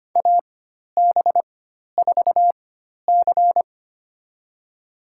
Morse code, heard as a keyed tone, about 25 wpm, spaced as for 12 wpm, 700 Hz. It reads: AB4C
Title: USA Call Signs - 4x Character Spacing - 25wpm